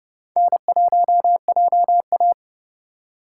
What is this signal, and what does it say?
Morse code, heard as a keyed tone, about 30 wpm, 700 Hz. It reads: N1JA
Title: USA Call Signs - 4 Characters - 4x Character Spacing - 30wpm